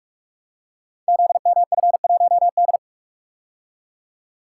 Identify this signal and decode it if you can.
Morse code, heard as a keyed tone, about 46 wpm, 700 Hz. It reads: ZMF1D
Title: Code Groups - 5 Letter:Numbers - 45wpm